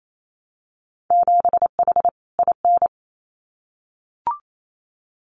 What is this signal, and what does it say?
Morse code, heard as a keyed tone, about 28 wpm, 700 Hz. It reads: 7H IN
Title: Winter Field Day Exchanges Encore 28wpm